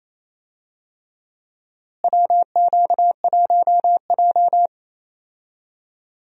Morse code, heard as a keyed tone, about 28 wpm, 700 Hz. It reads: WQ1J
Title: USA Call Signs - 4 Characters - Repeated 3x - 28wpm